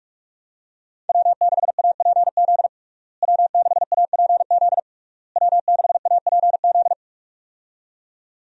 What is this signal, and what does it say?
Morse code, heard as a keyed tone, about 46 wpm, 700 Hz. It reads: W6APZ W6APZ W6APZ
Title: USA Call Signs - 5 Characters - Repeated 3x - 45wpm